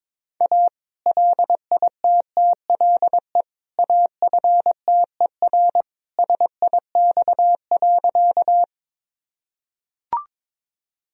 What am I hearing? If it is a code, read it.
Morse code, heard as a keyed tone, about 22 wpm, 700 Hz. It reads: A LITTLE AFTER SIX.
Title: Sentences from Top 1000 Words - Review Part 2 - 22wpm